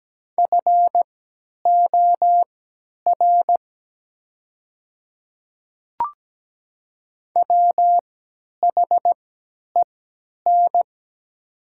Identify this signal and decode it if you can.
Morse code, heard as a keyed tone, about 17 wpm, spaced as for 10 wpm, 700 Hz. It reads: FOR WHEN
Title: 3x Character Spacing - Top 100 Words - 17wpm